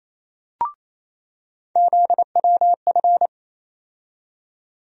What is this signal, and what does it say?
Morse code, heard as a keyed tone, about 28 wpm, 700 Hz. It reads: ZWF
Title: USA Call Sign Postfixes - 3 Letters 28wpm